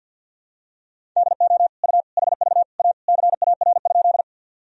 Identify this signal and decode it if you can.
Morse code, heard as a keyed tone, about 52 wpm, 700 Hz. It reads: DO U HV A CAR?
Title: Sentences from Top 300 Words and Abbreviations 50wpm